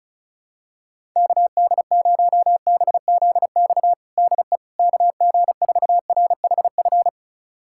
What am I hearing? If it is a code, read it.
Morse code, heard as a keyed tone, about 34 wpm, 700 Hz. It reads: KD0BZX DE KG4RHF
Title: USA Callsign DE Callsign - 6 Characters - 35wpm